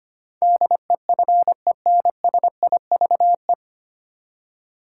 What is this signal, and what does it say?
Morse code, heard as a keyed tone, about 25 wpm, 700 Hz. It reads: DEFENSIVE
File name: Nine Letter Words - 25wpm